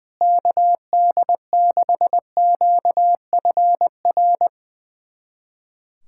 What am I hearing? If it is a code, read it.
Morse code, heard as a keyed tone, about 21 wpm, 700 Hz. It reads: KD6QFR
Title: USA Callsign DE Callsign 20wpm